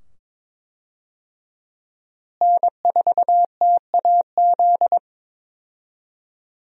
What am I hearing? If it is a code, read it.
Morse code, heard as a keyed tone, about 22 wpm, 700 Hz. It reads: N4TAZ
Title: USA Call Signs - 5 Characters - Repeated 2x - 22wpm